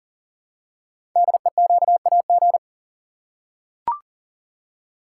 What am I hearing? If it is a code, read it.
Morse code, heard as a keyed tone, about 38 wpm, 700 Hz. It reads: DEQAG
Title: Code Groups - 5 Letters - 40wpm